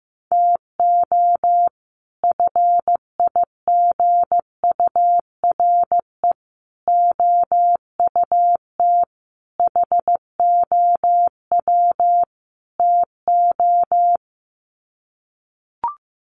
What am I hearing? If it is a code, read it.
Morse code, heard as a keyed tone, about 15 wpm, 700 Hz. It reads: TO FIGURE OUT HOW TO
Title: Sets of 5 Words 15wpm